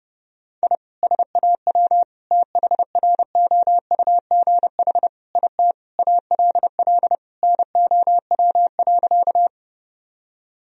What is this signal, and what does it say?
Morse code, heard as a keyed tone, about 32 wpm, 700 Hz. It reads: I SAW THROUGH IT ALL NOW.
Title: Sentences from Top 1000 Words - Review Part 1 - 30wpm